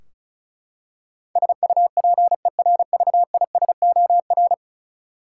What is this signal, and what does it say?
Morse code, heard as a keyed tone, about 34 wpm, 700 Hz. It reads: SUPERVISOR
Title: Practice QSO - Exchange 3 - More Info - 35wpm